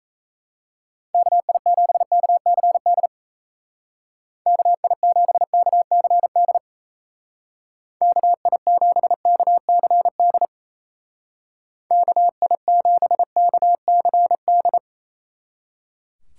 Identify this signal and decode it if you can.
Morse code, heard as a keyed tone, about 35 wpm, 700 Hz. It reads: KI7KCD KI7KCD KI7KCD KI7KCD
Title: Speed Racing - USA 2x3 Call Signs 28wpm